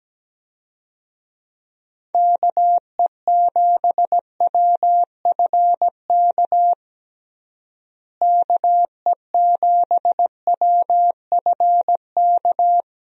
Morse code, heard as a keyed tone, about 17 wpm, 700 Hz. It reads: KE7WFK KE7WFK
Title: USA Call Signs - 6 Characters - Repeated 2x - 17wpm